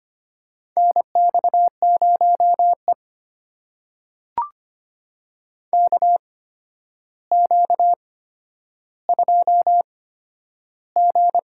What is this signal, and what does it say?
Morse code, heard as a keyed tone, about 25 wpm, spaced as for 11 wpm, 700 Hz. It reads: NX0E KQ2G
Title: USA Call Signs - 4 Characters - 8x Character Spacing - 25wpm